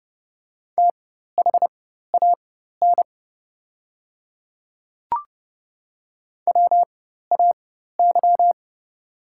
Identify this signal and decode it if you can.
Morse code, heard as a keyed tone, about 30 wpm, spaced as for 14 wpm, 700 Hz. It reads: THAN WAY
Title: 4x Character Spacing - Top 100 Words - 30wpm